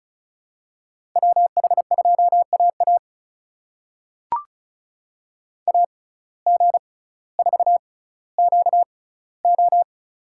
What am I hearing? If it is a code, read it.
Morse code, heard as a keyed tone, about 34 wpm, spaced as for 18 wpm, 700 Hz. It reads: WH2AA AG4QO
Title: USA Call Signs - 5 Characters - 6x Character Spacing - 35wpm